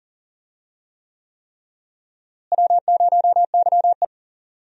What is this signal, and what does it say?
Morse code, heard as a keyed tone, about 40 wpm, 700 Hz. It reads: W0YE
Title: USA Call Signs - 4 Characters - Repeated 3x - 40wpm